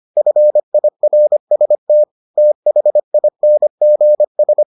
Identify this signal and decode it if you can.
Morse code, heard as a keyed tone, about 25 wpm, 600 Hz. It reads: FIRST THINGS